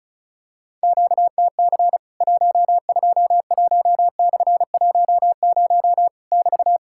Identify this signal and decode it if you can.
Morse code, heard as a keyed tone, about 36 wpm, 700 Hz. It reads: QTC 121/10 <BT>
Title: Worked All Europe DX Contest - QTC Exchange - 35wpm